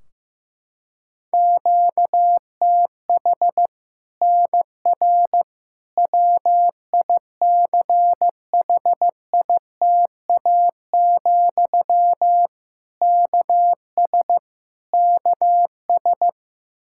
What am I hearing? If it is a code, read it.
Morse code, heard as a keyed tone, about 15 wpm, 700 Hz. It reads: QTH NR WICHITA, KS KS
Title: Practice QSO - Exchange 1 - Beginning - 15wpm